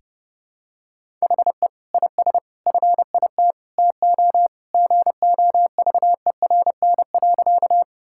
Morse code, heard as a keyed tone, about 30 wpm, 700 Hz. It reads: HE IS FIT TO GOVERN.
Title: Sentences from Top 1000 Words - Review Part 5 - 30wpm